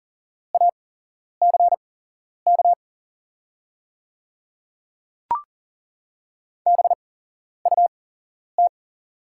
Morse code, heard as a keyed tone, about 40 wpm, spaced as for 11 wpm, 690 Hz. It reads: ACK BUT
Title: 8x Character Spacing - Top 500 Words - 40wpm